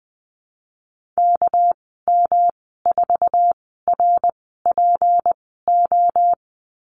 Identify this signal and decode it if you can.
Morse code, heard as a keyed tone, about 20 wpm, spaced as for 15 wpm, 700 Hz. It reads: KM4RPO
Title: 207 Lesson 33g - Letter j - Call Signs 20wpm